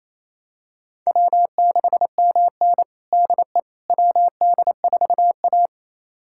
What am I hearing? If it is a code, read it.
Morse code, heard as a keyed tone, about 28 wpm, 700 Hz. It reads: W6MN DE WD4A
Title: USA Callsign DE Callsign - 4 Characters - 28wpm